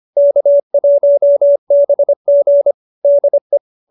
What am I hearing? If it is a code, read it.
Morse code, heard as a keyed tone, about 25 wpm, 570 Hz. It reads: K1BG DE
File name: qso201b 25